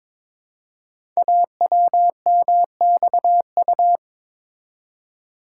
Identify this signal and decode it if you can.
Morse code, heard as a keyed tone, about 23 wpm, 700 Hz. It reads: AWMXU
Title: Code Groups - 5 Letters - 22wpm